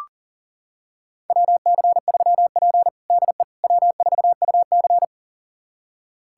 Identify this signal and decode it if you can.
Morse code, heard as a keyed tone, about 40 wpm, 700 Hz. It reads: WC3P DE W4UC
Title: USA Callsign DE Callsign - 4 Characters - 40wpm